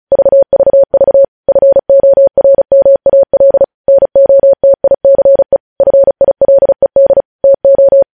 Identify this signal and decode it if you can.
Morse code, heard as a keyed tone, about 36 wpm, 550 Hz. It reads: VVV FORMAL NOTICE FILED TO
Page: News headlines in Morse code at 35 wpm